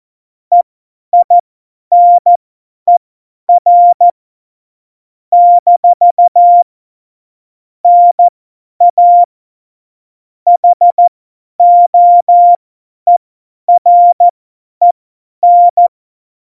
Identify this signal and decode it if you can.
Morse code, heard as a keyed tone, about 14 wpm, spaced as for 10 wpm, 700 Hz. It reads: EINER - NA HOEREN
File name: Morsekurs / Witze / Witz 22. Oktober 2019, 50 F70